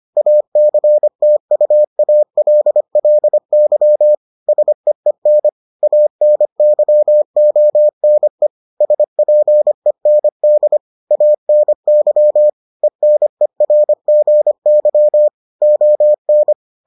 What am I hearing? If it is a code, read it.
Morse code, heard as a keyed tone, about 25 wpm, 600 Hz. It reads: ACTUALLY SEEN ANYONE SPEND ANY ENERGY ON